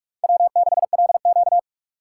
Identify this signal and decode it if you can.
Morse code, heard as a keyed tone, about 42 wpm, 700 Hz. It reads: W6LQ
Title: USA Call Signs - 4 Characters - 2x Character Spacing - 45wpm